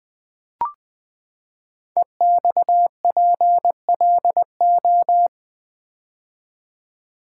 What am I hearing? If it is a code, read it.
Morse code, heard as a keyed tone, about 20 wpm, 700 Hz. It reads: EXPLO
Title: Build A Word - 5 Letters At A Time - 12-Letter Words - 20wpm